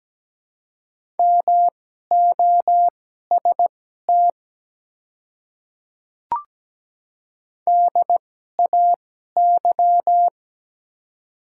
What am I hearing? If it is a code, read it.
Morse code, heard as a keyed tone, about 17 wpm, spaced as for 12 wpm, 700 Hz. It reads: MOST DAY
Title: 2x Character Spacing - Top 100 Words - 17wpm